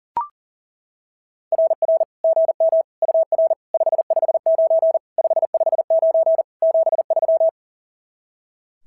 Morse code, heard as a keyed tone, about 40 wpm, 660 Hz. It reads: RR GM UR 559 559 73
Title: SOTA QSO Practice 40wpm